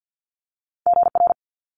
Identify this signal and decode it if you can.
Morse code, heard as a keyed tone, about 50 wpm, 700 Hz. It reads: GR
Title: Multi-letter Phonemes Quick Fire 50wpm